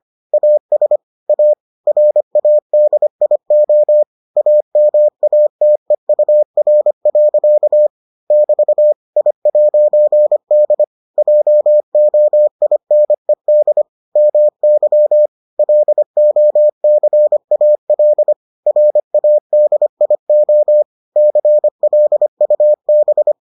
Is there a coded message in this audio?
Yes: Morse code from 0.3 to 23.3 s, heard as a keyed tone, about 25 wpm, 600 Hz. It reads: AS A RADIO AMATEUR. <BT> I'D JOINED MY LOCAL RADIO CLUB